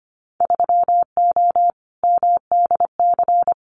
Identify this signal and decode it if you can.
Morse code, heard as a keyed tone, about 25 wpm, 700 Hz. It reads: 3O MDC
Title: Winter Field Day Exchanges Encore 25wpm